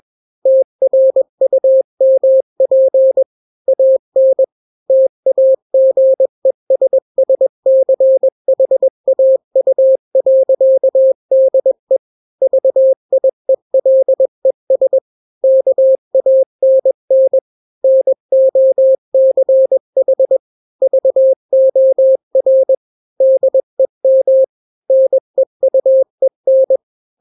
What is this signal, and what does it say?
Morse code, heard as a keyed tone, about 21 wpm, 530 Hz. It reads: TRUMP AN TAGESSCHAU.DE VIELES KANN NOCH VOR DEM NEUEN
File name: daily / 20wpm